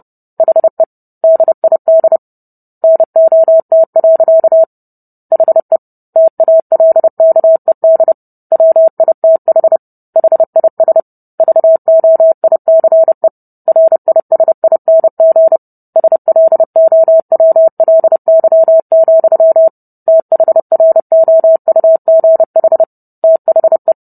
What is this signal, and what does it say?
Morse code, heard as a keyed tone, about 30 wpm, 650 Hz. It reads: HE DID NOT. HE TALKED WITH HIS VOICE RISING SLOWLY, THROUGH THE